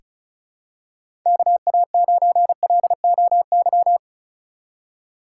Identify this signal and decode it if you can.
Morse code, heard as a keyed tone, about 34 wpm, 700 Hz. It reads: KA9LOY
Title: USA Call Signs - 6 Characters - 4x Character Spacing - 35wpm